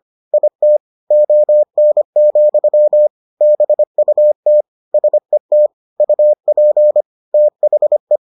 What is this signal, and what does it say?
Morse code, heard as a keyed tone, about 25 wpm, 600 Hz. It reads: IT ON, BUT SET UP THE